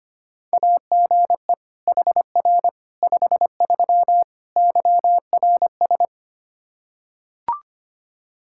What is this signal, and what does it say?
Morse code, heard as a keyed tone, about 25 wpm, 700 Hz. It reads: AGE HR 53 YRS
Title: Practice QSO - Exchange 3 - More Info - 25wpm